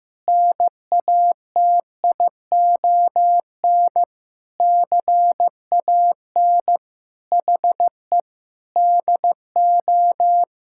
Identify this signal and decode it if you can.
Morse code, heard as a keyed tone, about 15 wpm, 700 Hz. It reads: NATION CAN HE DO
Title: Sentences from Top 1000 Words - Review Part 6 - 15wpm